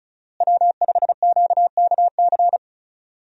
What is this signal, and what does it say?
Morse code, heard as a keyed tone, about 36 wpm, 700 Hz. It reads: W5QKC
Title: USA Call Signs - 5 Characters - Repeated 3x - 35wpm